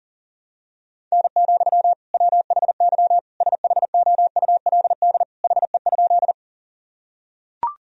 Morse code, heard as a keyed tone, about 40 wpm, 700 Hz. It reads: N, WHY SHOULD HE?
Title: Sentences from Top 300 Words and Abbreviations 40wpm